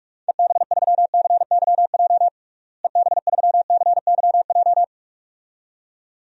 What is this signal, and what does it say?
Morse code, heard as a keyed tone, about 45 wpm, 700 Hz. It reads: EB3CYJ EB3CYJ
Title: DXCC-100 Call Signs - 6 Characters - Repeated 2x - 45wpm